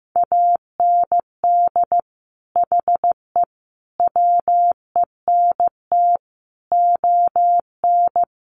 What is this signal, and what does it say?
Morse code, heard as a keyed tone, about 15 wpm, 700 Hz. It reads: AND HE WENT ON
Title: Sentences from Top 800 Words 15wpm